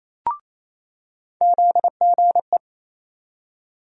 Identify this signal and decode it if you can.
Morse code, heard as a keyed tone, about 28 wpm, 700 Hz. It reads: ZGE
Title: USA Call Sign Postfixes - 3 Letters 28wpm